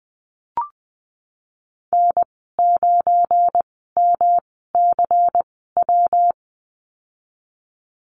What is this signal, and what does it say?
Morse code, heard as a keyed tone, about 20 wpm, spaced as for 14 wpm, 700 Hz. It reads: N9MCW
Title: 173 Lesson 28g - Number 9 - Call Signs 20wpm